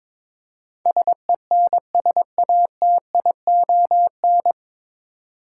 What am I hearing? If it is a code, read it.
Morse code, heard as a keyed tone, about 22 wpm, 700 Hz. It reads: SENSATION
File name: 9 Letter Words - First 3 Letters Given - 22wpm